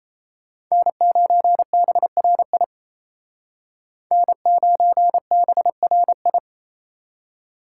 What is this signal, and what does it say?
Morse code, heard as a keyed tone, about 36 wpm, 700 Hz. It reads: N9BRI N9BRI